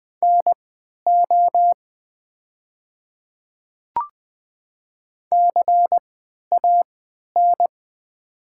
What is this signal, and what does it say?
Morse code, heard as a keyed tone, about 20 wpm, spaced as for 11 wpm, 700 Hz. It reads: NO CAN
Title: 3x Character Spacing - Top 100 Words - 20wpm